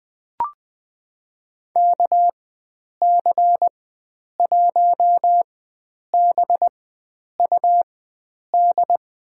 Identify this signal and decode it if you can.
Morse code, heard as a keyed tone, about 20 wpm, spaced as for 9 wpm, 700 Hz. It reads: KC1BUD
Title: USA Call Signs - 4x Character Spacing - 20wpm